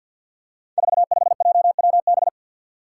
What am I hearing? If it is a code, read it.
Morse code, heard as a keyed tone, about 51 wpm, 700 Hz. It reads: 45JWB
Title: Five Challenging Letter-Number Combinations 50wpm